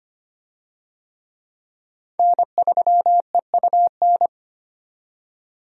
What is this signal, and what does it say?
Morse code, heard as a keyed tone, about 25 wpm, 700 Hz. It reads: N3EUN
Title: USA Call Signs - 5 Characters - 25wpm